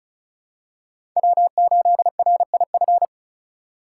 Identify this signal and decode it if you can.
Morse code, heard as a keyed tone, about 34 wpm, 700 Hz. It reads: W8RIF
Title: USA Call Signs - 2x Character Spacing - 35wpm